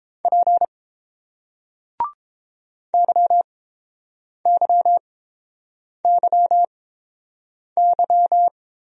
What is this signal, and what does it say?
Morse code, heard as a keyed tone, about 30 wpm, spaced as for 7 wpm, 700 Hz. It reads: P YYYY